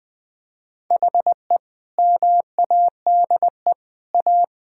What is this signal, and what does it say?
Morse code, heard as a keyed tone, about 21 wpm, 700 Hz. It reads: HE MADE A
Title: Sentences from Top 1600 Words 20wpm